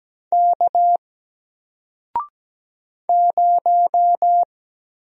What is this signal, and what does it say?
Morse code, heard as a keyed tone, about 17 wpm, 700 Hz. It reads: K 0
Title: Single Letter-Number Alt 17wpm